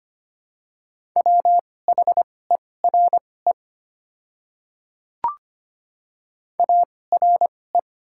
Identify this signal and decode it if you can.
Morse code, heard as a keyed tone, about 25 wpm, spaced as for 18 wpm, 700 Hz. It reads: WHERE ARE
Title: 2x Character Spacing - Top 300 Words - 25wpm